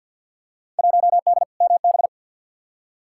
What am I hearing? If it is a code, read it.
Morse code, heard as a keyed tone, about 52 wpm, 700 Hz. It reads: 1D MB
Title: Field Day Exchanges Encore - 50wpm